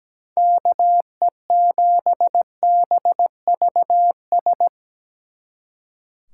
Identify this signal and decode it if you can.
Morse code, heard as a keyed tone, about 19 wpm, 700 Hz. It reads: KE7BVS
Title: Speed Racing - USA Call Signs 17wpm